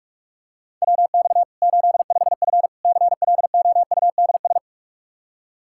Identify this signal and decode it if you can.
Morse code, heard as a keyed tone, about 42 wpm, 700 Hz. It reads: WX 85F CLOUDS
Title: Practice QSO - Exchange 2 - Info - 45wpm